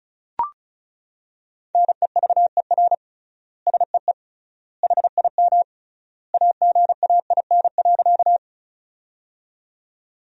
Morse code, heard as a keyed tone, about 36 wpm, 700 Hz. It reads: NEVER SEE HIM AGAIN.
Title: 3x Word-Spacing - Sentences from Top 300 Words - 35wpm